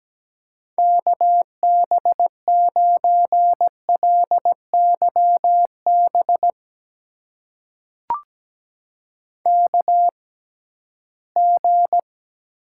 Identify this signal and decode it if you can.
Morse code, heard as a keyed tone, about 17 wpm, spaced as for 13 wpm, 700 Hz. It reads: KB9LYB KG
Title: USA Call Signs - 6 Characters - 6x Character Spacing - 17wpm